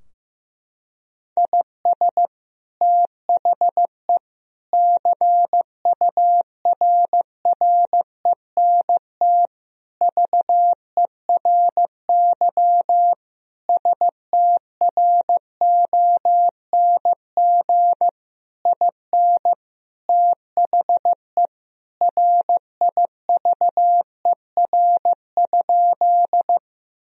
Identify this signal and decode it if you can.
Morse code, heard as a keyed tone, about 15 wpm, 700 Hz. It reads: IS THE CURRENT VERY STRONG IN THE RIVER?
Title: Sentences from Top 1000 Words - Review Part 4 - 15wpm